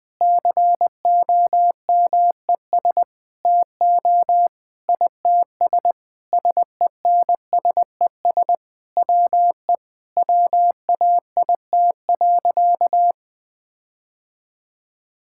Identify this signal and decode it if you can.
Morse code, heard as a keyed tone, about 20 wpm, 700 Hz. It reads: COMES TO ITS SENSES WE WAIT.
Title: Sentences with 9-Word Phrases - 20wpm